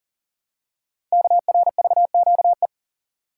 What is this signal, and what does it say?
Morse code, heard as a keyed tone, about 39 wpm, 700 Hz. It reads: KRVQE
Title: Code Groups - 5 Letters - 40wpm